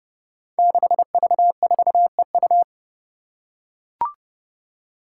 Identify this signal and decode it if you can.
Morse code, heard as a keyed tone, about 28 wpm, 700 Hz. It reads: 6V4EU
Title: Five Challenging Letter-Number Combinations 30wpm